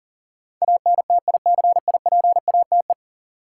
Morse code, heard as a keyed tone, about 40 wpm, 700 Hz. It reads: ANTICIPATE
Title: Sentences from Top 1500 Words - New Word Drill - 40wpm